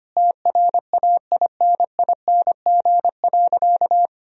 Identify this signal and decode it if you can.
Morse code, heard as a keyed tone, about 25 wpm, 700 Hz. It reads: TRAINING.
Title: Sentences from Top 1000 Words - Review Part 4 - 25wpm